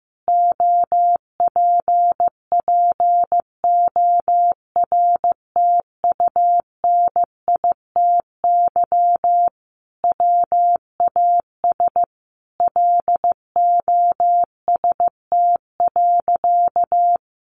Morse code, heard as a keyed tone, about 15 wpm, 700 Hz. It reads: OPPORTUNITY WAS LOST.
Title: Sentences from Top 1400 Words 15wpm